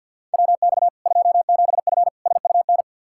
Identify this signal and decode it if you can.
Morse code, heard as a keyed tone, about 49 wpm, 700 Hz. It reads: WX 27F SUN
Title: Practice QSO - Exchange 2 - Info - 50wpm